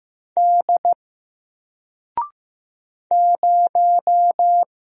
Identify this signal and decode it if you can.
Morse code, heard as a keyed tone, about 15 wpm, 700 Hz. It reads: D 0
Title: Single Letter-Number Alt 15wpm